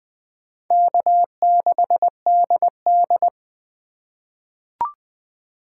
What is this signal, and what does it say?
Morse code, heard as a keyed tone, about 20 wpm, 700 Hz. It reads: K6DD
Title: USA Call Signs - 4 Characters - 2x Character Spacing - 20wpm